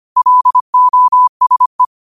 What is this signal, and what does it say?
Morse code, heard as a keyed tone, about 25 wpm, 1000 Hz. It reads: LOSE
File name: qotdfiles / 25wpm